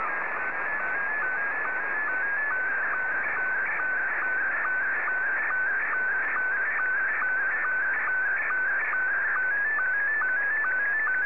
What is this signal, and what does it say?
SSTV (slow-scan television), Scottie 1.